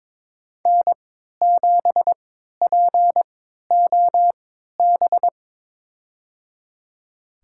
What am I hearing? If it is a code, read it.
Morse code, heard as a keyed tone, about 21 wpm, spaced as for 12 wpm, 700 Hz. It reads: N7POB
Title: USA Call Signs - 5 Characters - 3x Character Spacing - 22wpm